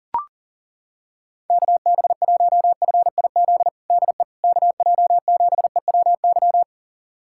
Morse code, heard as a keyed tone, about 40 wpm, 700 Hz. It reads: KB1FIZ DE KJ7EWY
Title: USA Callsign DE Callsign 40wpm